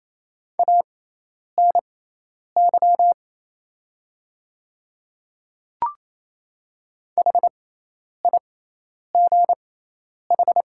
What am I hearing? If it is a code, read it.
Morse code, heard as a keyed tone, about 28 wpm, spaced as for 10 wpm, 700 Hz. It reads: ANY HIGH
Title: 6x Character Spacing - Top 200 Words - 28wpm